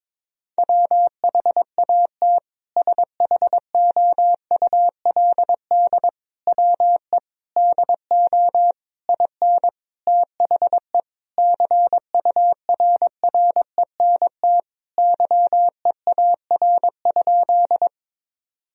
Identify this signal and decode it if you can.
Morse code, heard as a keyed tone, about 22 wpm, 700 Hz. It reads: WHAT SHOULD WE DO IN THE CURRENT YEAR?
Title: Sentences from Top 1000 Words - Review Part 4 - 22wpm